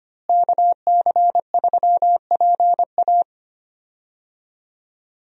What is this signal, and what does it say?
Morse code, heard as a keyed tone, about 25 wpm, 700 Hz. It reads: KC3PA
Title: USA Call Signs - 5 Characters - Repeated 3x - 25wpm